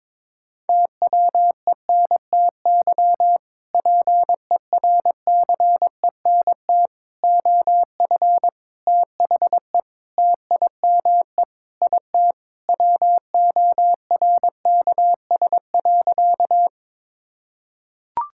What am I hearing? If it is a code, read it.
Morse code, heard as a keyed tone, about 22 wpm, 700 Hz. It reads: TWENTY PERCENT OF THE TIME IT WORKS.
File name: practice / 22wpm / Sentences with 2-Word Phrases - 22wpm